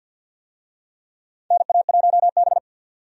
Morse code, heard as a keyed tone, about 50 wpm, 700 Hz. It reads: NA1B
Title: USA Call Signs - 4 Characters - 50wpm